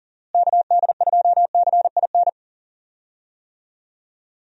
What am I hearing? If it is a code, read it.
Morse code, heard as a keyed tone, about 40 wpm, 700 Hz. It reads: KD2CIN
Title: USA Call Signs - 6 Characters - Repeated 3x - 40wpm